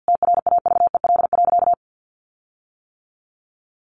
Morse code, heard as a keyed tone, about 50 wpm, 700 Hz. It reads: TRAVEL.
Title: Sentences from Top 1300 Words 50wpm